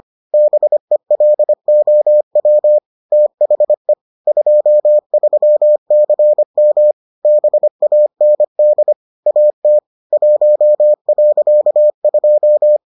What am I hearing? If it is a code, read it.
Morse code, heard as a keyed tone, about 25 wpm, 600 Hz. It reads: BELOW THE 23CM BAND AT 1.2